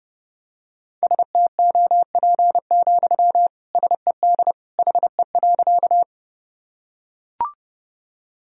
Morse code, heard as a keyed tone, about 32 wpm, 700 Hz. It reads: STOP, SED HE.
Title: Sentences from Top 300 Words and Abbreviations 30wpm